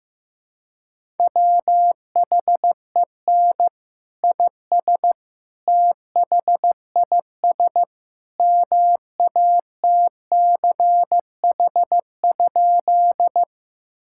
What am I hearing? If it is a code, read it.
Morse code, heard as a keyed tone, about 16 wpm, 700 Hz. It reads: WHEN IS THIS MATCH?
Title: Sentences from Top 1600 Words 15wpm